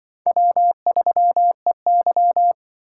Morse code, heard as a keyed tone, about 25 wpm, 700 Hz. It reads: W3EY
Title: Speed Racing - USA 1x2 Call Signs 20wpm